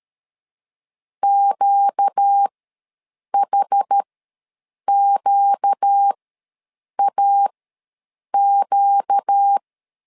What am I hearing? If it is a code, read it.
Morse code, heard as a keyed tone, about 13 wpm, spaced as for 7 wpm, 800 Hz. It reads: QHQAQ
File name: LEZIONI / 4 Q ws 12wpm 1912